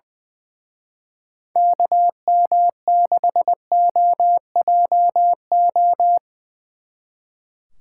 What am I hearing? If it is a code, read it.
Morse code, heard as a keyed tone, about 22 wpm, 700 Hz. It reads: KM6OJO
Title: Speed Racing - USA 2x3 Call Signs 20wpm